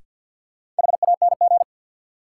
Morse code, heard as a keyed tone, about 50 wpm, 700 Hz. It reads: HANG
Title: Sentences from Top 1600 Words - New Word Drill - 50wpm